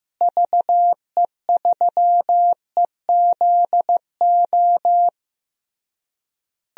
Morse code, heard as a keyed tone, about 15 wpm, 700 Hz. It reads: VE3EZO
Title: DXCC-100 Call Signs - 6 Characters - Repeated 3x - 15wpm